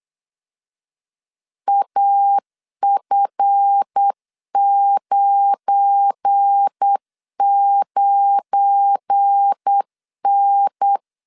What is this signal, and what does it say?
Morse code, heard as a keyed tone, about 8 wpm, 800 Hz. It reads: AF99N
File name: LEZIONI / 7 GF9Z / 7 GF9 8wpm 3545